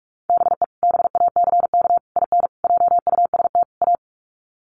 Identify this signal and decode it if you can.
Morse code, heard as a keyed tone, about 44 wpm, 700 Hz. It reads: BE BACK IN JUST A
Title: Sets of 5 Words Encore 45wpm